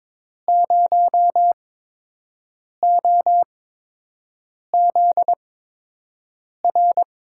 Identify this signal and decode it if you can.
Morse code, heard as a keyed tone, about 22 wpm, spaced as for 6 wpm, 700 Hz. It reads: 0OZR
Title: USA Call Signs - 6 Characters - 8x Character Spacing - 22wpm